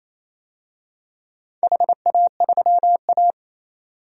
Morse code, heard as a keyed tone, about 28 wpm, 700 Hz. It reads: HA3A
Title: DXCC-100 Call Signs - 4 Characters - 28wpm